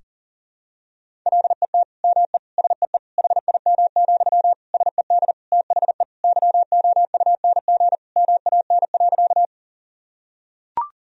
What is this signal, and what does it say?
Morse code, heard as a keyed tone, about 40 wpm, 700 Hz. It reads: LET ME SEE HIM, SED THE YOUNG MAN.
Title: Sentences from Top 400 Words and Abbreviations 40wpm